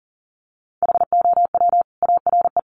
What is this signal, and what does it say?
Morse code, heard as a keyed tone, about 40 wpm, 700 Hz. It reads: HOW ARE
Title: Sentences from Top 100 Words 40wpm